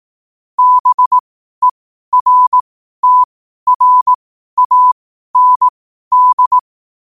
Morse code, heard as a keyed tone, about 18 wpm, spaced as for 12 wpm, 1000 Hz. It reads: BERTRAND
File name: qotdfiles / 12wpm